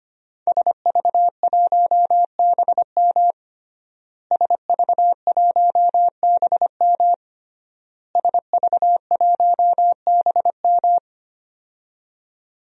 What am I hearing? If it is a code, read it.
Morse code, heard as a keyed tone, about 25 wpm, 700 Hz. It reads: SV1BM SV1BM SV1BM
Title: DXCC-100 Call Signs - 5 Characters - Repeated 3x - 25wpm